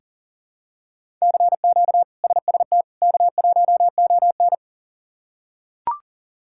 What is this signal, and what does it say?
Morse code, heard as a keyed tone, about 37 wpm, 690 Hz. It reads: CQ SST K1ON
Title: K1USN SST QSO Practice - 40wpm